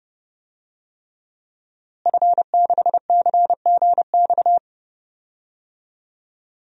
Morse code, heard as a keyed tone, about 30 wpm, 700 Hz. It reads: F6CGX
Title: DXCC-100 Call Signs - 5 Characters - Repeated 3x - 30wpm